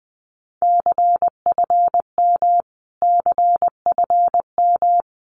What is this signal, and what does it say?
Morse code, heard as a keyed tone, about 20 wpm, 700 Hz. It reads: CFM CFM
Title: Abbreviations Repeated 3x - 20wpm